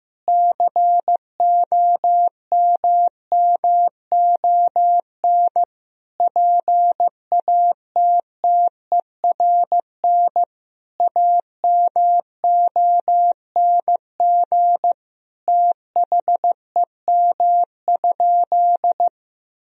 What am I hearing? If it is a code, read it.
Morse code, heard as a keyed tone, about 15 wpm, 700 Hz. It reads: COMMON PATTERN AMONG THEM?
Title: Sentences from Top 1000 Words - Review Part 2 - 15wpm